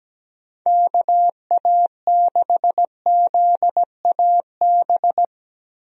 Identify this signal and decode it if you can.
Morse code, heard as a keyed tone, about 17 wpm, 700 Hz. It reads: KA6ZAB
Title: USA Call Signs - 6 Characters - Repeated 2x - 17wpm